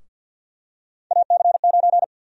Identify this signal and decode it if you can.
Morse code, heard as a keyed tone, about 52 wpm, 700 Hz. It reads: AC9
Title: USA Call Sign Prefixes - Repeated 2x - 50wpm